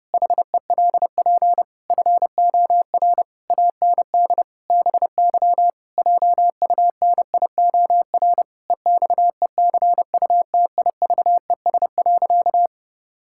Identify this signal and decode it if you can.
Morse code, heard as a keyed tone, about 30 wpm, 700 Hz. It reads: HELP FOR AND BY JUNIOR EXECUTIVES.
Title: Pangram Challenge v2 - 30wpm